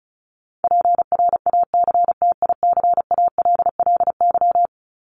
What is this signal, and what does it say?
Morse code, heard as a keyed tone, about 34 wpm, 700 Hz. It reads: PRACTICALLY
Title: LY Suffix - Top 100 Words with Base Words 35wpm